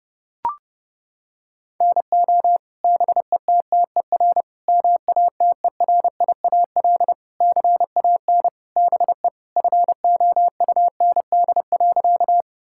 Morse code, heard as a keyed tone, about 32 wpm, 700 Hz. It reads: NO BETTER MATERIAL CAN BE FOUND.
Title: Sentences from Top 1000 Words - Review Part 5 - 30wpm